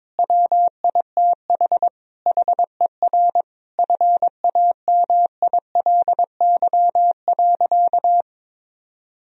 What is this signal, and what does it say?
Morse code, heard as a keyed tone, about 22 wpm, 700 Hz. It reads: WITH HER FAMILY.
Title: Sentences from Top 1000 Words - Review Part 4 - 22wpm